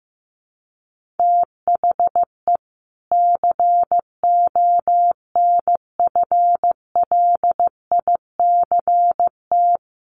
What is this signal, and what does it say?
Morse code, heard as a keyed tone, about 15 wpm, 700 Hz. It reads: THE CONFLICT
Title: Sentences from Top 1300 Words 15wpm